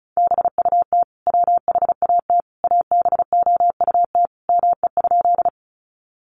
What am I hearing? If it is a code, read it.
Morse code, heard as a keyed tone, about 34 wpm, 700 Hz. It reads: BUT WHAT ABOUT ME?
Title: Sentences from Top 200 Words 35wpm